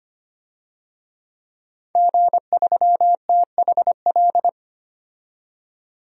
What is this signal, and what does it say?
Morse code, heard as a keyed tone, about 25 wpm, 700 Hz. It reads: G3THL
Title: DXCC-100 Call Signs - 5 Characters - 25wpm